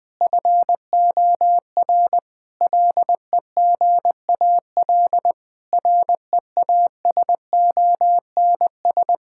Morse code, heard as a keyed tone, about 21 wpm, 700 Hz. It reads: FOR LEGAL REASONS
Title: Sets of 8 Words - 20wpm